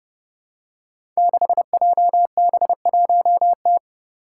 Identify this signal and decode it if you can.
Morse code, heard as a keyed tone, about 28 wpm, 700 Hz. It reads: 6JB1T
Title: Five Challenging Letter-Number Combinations 30wpm